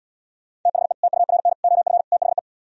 Morse code, heard as a keyed tone, about 77 wpm, 700 Hz. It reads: THE TURN OF THE